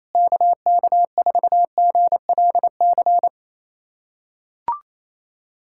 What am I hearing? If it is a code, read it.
Morse code, heard as a keyed tone, about 28 wpm, 700 Hz. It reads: KK4GLC
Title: USA Callsign DE Callsign 28wpm